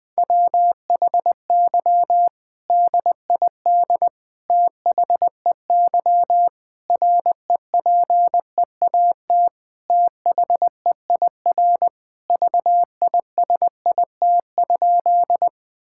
Morse code, heard as a keyed tone, about 20 wpm, 700 Hz. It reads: WHY DID THEY REPEAT THEIR VISIT?
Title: Sentences from Top 1600 Words 20wpm